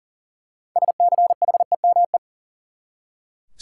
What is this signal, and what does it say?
Morse code, heard as a keyed tone, about 40 wpm, 700 Hz. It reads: SCHEME
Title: Six Letter Words - 40wpm